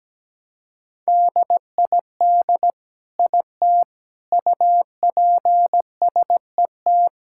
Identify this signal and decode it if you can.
Morse code, heard as a keyed tone, about 17 wpm, 700 Hz. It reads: DID IT UPSET
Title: Sentences from Top 1000 Words - Review Part 5 - 17wpm